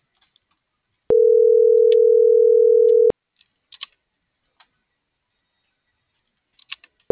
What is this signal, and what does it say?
Ringback tone (North America), 440 + 480 Hz.